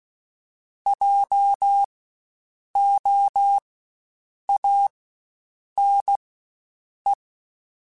Morse code, heard as a keyed tone, about 16 wpm, spaced as for 7 wpm, 800 Hz. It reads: JOANE
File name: Spaced Alphas / Spaced Alphas mp3s / Js sp @15